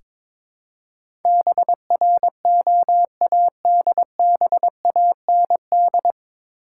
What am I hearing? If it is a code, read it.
Morse code, heard as a keyed tone, about 22 wpm, 700 Hz. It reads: BROADBAND
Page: Morse Code Ninja practice 22wpm